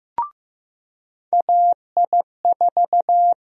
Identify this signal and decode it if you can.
Morse code, heard as a keyed tone, about 15 wpm, 700 Hz. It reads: AI4
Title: USA Call Sign Prefixes - 15wpm